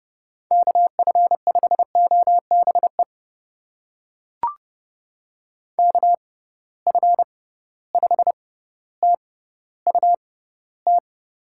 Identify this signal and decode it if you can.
Morse code, heard as a keyed tone, about 30 wpm, spaced as for 15 wpm, 700 Hz. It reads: KF5OBE KF5TUT
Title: USA Call Signs - 6 Characters - 6x Character Spacing - 30wpm